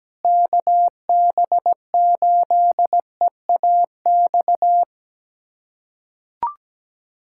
Morse code, heard as a keyed tone, about 17 wpm, 700 Hz. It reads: KB8EAX
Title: USA Callsign DE Callsign - 6 Characters - 17wpm